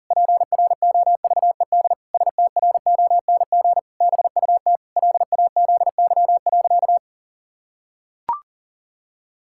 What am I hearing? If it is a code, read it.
Morse code, heard as a keyed tone, about 40 wpm, 700 Hz. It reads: PROVED STRONG BUT LAZY.
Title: Pangram Challenge v2 - 40wpm